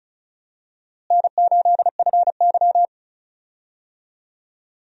Morse code, heard as a keyed tone, about 36 wpm, 700 Hz. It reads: N8FY
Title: USA Call Signs - 4 Characters - Repeated 3x - 35wpm